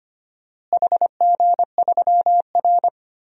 Morse code, heard as a keyed tone, about 25 wpm, 700 Hz. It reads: HG3R
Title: DXCC-100 Call Signs - 4 Characters - Repeated 2x - 25wpm